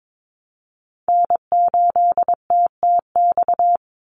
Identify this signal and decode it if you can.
Morse code, heard as a keyed tone, about 22 wpm, 700 Hz. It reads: N8TTX
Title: USA Call Signs 22wpm